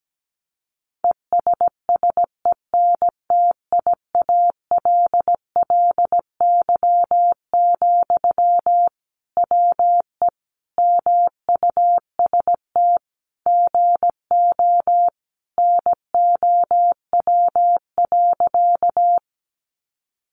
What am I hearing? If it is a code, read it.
Morse code, heard as a keyed tone, about 17 wpm, 700 Hz. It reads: ESSENTIALLY, WE MUST GO NOW.
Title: Sentences from Top 1300 Words 17wpm